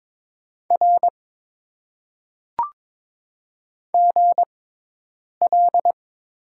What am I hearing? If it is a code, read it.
Morse code, heard as a keyed tone, about 22 wpm, spaced as for 8 wpm, 700 Hz. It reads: R GL
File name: Top 90 QSO Elements - 6x Character Spacing - 22wpm